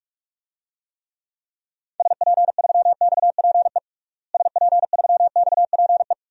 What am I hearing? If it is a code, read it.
Morse code, heard as a keyed tone, about 47 wpm, 700 Hz. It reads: SP3XPE SP3XPE
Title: DXCC-100 Call Signs - 6 Characters - Repeated 2x - 45wpm